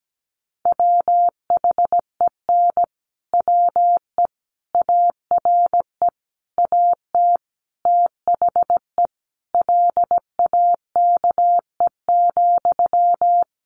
Morse code, heard as a keyed tone, about 17 wpm, 700 Hz. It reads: WHEN WE ARE AT THE LAKE,
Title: Sentences from Top 1100 Words 17wpm